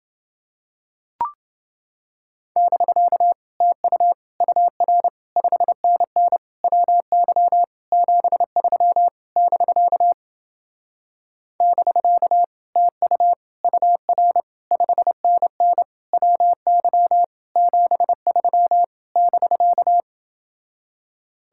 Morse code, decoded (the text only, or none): <BK> TU UR 5NN WY 73 <BK> <BK> TU UR 5NN WY 73 <BK>